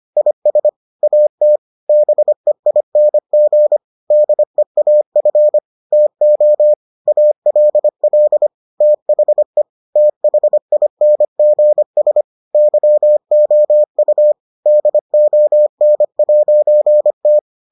Morse code, heard as a keyed tone, about 25 wpm, 600 Hz. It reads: IS AT BEING DEAF TO ALL THE THINGS YOU DON'T